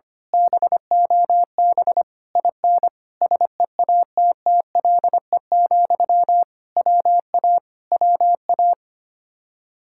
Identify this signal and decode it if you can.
Morse code, heard as a keyed tone, about 24 wpm, 700 Hz. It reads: BOB IN SEATTLE, WA WA
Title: Speed-Racing Common QSO Phrases Encore - 25wpm